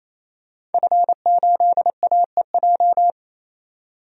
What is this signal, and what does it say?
Morse code, heard as a keyed tone, about 28 wpm, 700 Hz. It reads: F8AEJ
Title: DXCC-100 Call Signs - 5 Characters - 28wpm